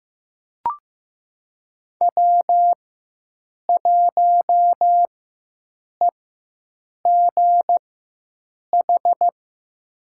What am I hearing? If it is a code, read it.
Morse code, heard as a keyed tone, about 15 wpm, spaced as for 7 wpm, 700 Hz. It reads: W1EGH
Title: USA Call Signs - 5 Characters - 4x Character Spacing - 15wpm